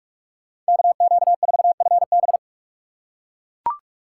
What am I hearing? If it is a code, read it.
Morse code, heard as a keyed tone, about 43 wpm, 700 Hz. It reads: KQ4FB
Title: USA Call Signs - 5 Characters - 8x Character Spacing - 45wpm